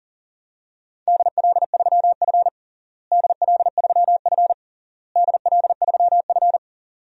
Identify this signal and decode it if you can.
Morse code, heard as a keyed tone, about 40 wpm, 700 Hz. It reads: DL3F DL3F DL3F
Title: DXCC-100 Call Signs - 4 Characters - Repeated 3x - 40wpm